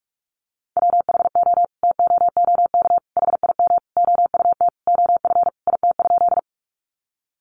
Morse code, heard as a keyed tone, about 44 wpm, 700 Hz. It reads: WHO TOOK HIM OUT OF IT?
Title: Sentences from Top 300 Words 45wpm